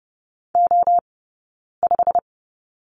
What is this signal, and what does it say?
Morse code, heard as a keyed tone, about 30 wpm, spaced as for 9 wpm, 690 Hz. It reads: O5
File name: course / 30:10wpm / 086 Lesson 15d - Number 5 - Two Characters 30wpm